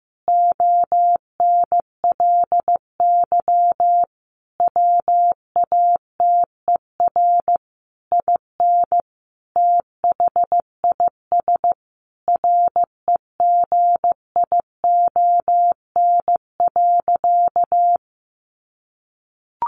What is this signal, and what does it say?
Morse code, heard as a keyed tone, about 15 wpm, 700 Hz. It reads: ONLY WATER IN THIS REGION.
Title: Sentences from Top 1500 Words 15wpm